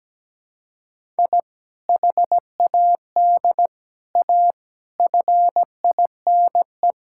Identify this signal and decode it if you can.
Morse code, heard as a keyed tone, about 17 wpm, 700 Hz. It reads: I HAD A FINE
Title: Sentences from Top 1000 Words - Review Part 6 - 17wpm